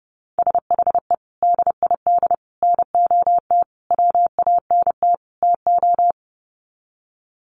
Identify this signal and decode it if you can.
Morse code, heard as a keyed tone, about 30 wpm, 700 Hz. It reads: SHE DID NOT WANT TO
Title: Sets of 5 Words 30wpm